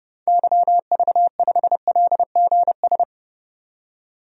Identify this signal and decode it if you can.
Morse code, heard as a keyed tone, about 30 wpm, 700 Hz. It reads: YV5LGS
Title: DXCC-100 Call Signs - 6 Characters - 30wpm